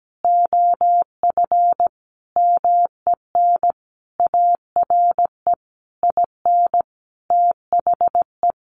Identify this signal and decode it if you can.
Morse code, heard as a keyed tone, about 17 wpm, 700 Hz. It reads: OF MEN ARE IN THE
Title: Sentences from Top 1300 Words 17wpm